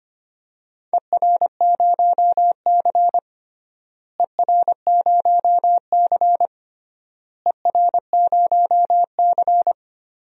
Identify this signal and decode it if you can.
Morse code, heard as a keyed tone, about 25 wpm, 700 Hz. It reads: ER0C ER0C ER0C
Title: DXCC-100 Call Signs - 4 Characters - Repeated 3x - 25wpm